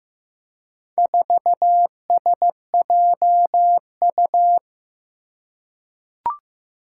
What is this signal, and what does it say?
Morse code, heard as a keyed tone, about 15 wpm, 700 Hz. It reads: 4SJU
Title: Four Challenging Letter-Number Combinations 15wpm